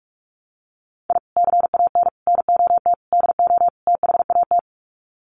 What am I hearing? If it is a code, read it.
Morse code, heard as a keyed tone, about 44 wpm, 700 Hz. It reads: I CAN NOT DO THAT
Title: Sets of 5 Words Encore2 45wpm